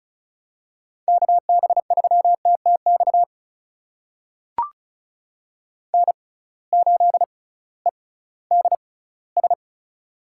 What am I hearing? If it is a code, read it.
Morse code, heard as a keyed tone, about 36 wpm, spaced as for 19 wpm, 700 Hz. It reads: KB3TTX N8EDS
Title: USA Call Signs - 6x Character Spacing - 35wpm